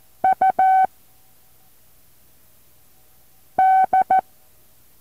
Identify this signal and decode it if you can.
Morse code, heard as a keyed tone, about 14 wpm, 760 Hz. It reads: U D
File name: intl morse code2